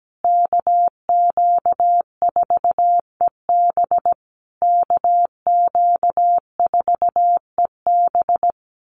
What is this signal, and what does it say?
Morse code, heard as a keyed tone, about 17 wpm, 700 Hz. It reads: KQ4EB KQ4EB
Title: USA Call Signs Repeated 3x 17wpm